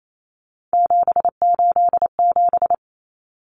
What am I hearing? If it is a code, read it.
Morse code, heard as a keyed tone, about 28 wpm, 700 Hz. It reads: 787
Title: Common Cut Numbers - 3-Digit 28wpm